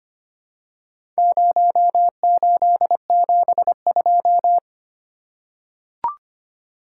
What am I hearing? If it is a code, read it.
Morse code, heard as a keyed tone, about 25 wpm, 700 Hz. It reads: 0872
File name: practice / 25wpm / Four Letter-Number Combinations - 25wpm